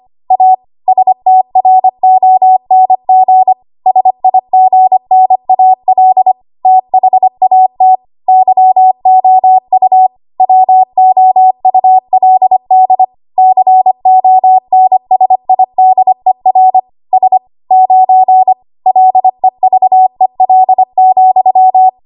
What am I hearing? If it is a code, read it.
Morse code, heard as a keyed tone, about 25 wpm, 750 Hz. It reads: A STRONG SIGNAL THAT YOU WOULD CONSIDER S 9 LEVEL,
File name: Archive / 25 WPM / 25WPM